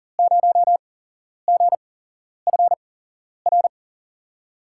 Morse code, heard as a keyed tone, about 40 wpm, spaced as for 11 wpm, 700 Hz. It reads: 0GFR
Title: USA Call Signs - 5 Characters - 8x Character Spacing - 40wpm